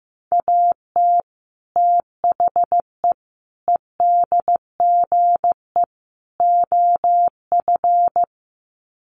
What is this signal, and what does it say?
Morse code, heard as a keyed tone, about 15 wpm, 700 Hz. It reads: AT THE EDGE OF
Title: Sets of 4 Words 15wpm